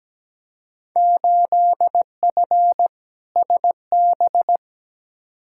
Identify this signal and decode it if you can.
Morse code, heard as a keyed tone, about 17 wpm, 700 Hz. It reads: 8F SB
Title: Field Day Exchanges Encore - 17wpm